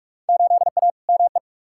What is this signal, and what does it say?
Morse code, heard as a keyed tone, about 48 wpm, 700 Hz. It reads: 8A ME
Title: Field Day Exchanges - 45wpm